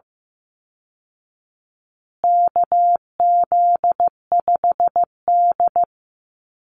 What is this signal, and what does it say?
Morse code, heard as a keyed tone, about 15 wpm, 700 Hz. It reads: KZ5D
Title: Call Signs Repeated 2x - 15wpm